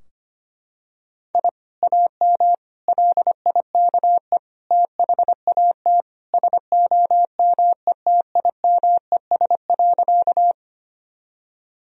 Morse code, heard as a keyed tone, about 25 wpm, 700 Hz. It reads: I AM LIKE THAT SOMETIMES.
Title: Sentences from Top 1000 Words - Review Part 3 - 25wpm